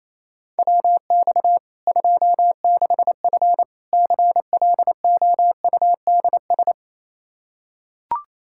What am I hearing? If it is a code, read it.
Morse code, heard as a keyed tone, about 28 wpm, 700 Hz. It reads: WX 26F CLOUDS
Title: Practice QSO - Exchange 2 - Info - 28wpm